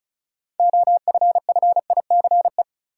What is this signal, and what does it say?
Morse code, heard as a keyed tone, about 36 wpm, 700 Hz. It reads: OFFICE
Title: Six Letter Words - 35wpm